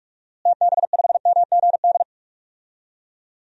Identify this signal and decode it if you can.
Morse code, heard as a keyed tone, about 46 wpm, 700 Hz. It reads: TB5MGD
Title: DXCC-100 Call Signs - 6 Characters - Repeated 2x - 45wpm